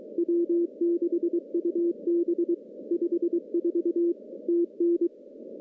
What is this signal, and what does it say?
Morse code, heard as a keyed tone, about 23 wpm, 340 Hz. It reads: W6UB 54 TN
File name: contests / CW-SS-k5tr- -1500